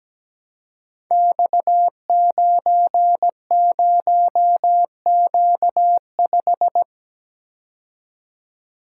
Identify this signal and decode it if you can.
Morse code, heard as a keyed tone, about 17 wpm, 700 Hz. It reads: X90Q5
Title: Code Groups - 5 Letter:Numbers - 17wpm